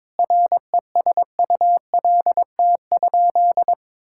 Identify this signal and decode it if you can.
Morse code, heard as a keyed tone, about 22 wpm, 700 Hz. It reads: RESULT?
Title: Sentences from Top 1000 Words - Review Part 5 - 22wpm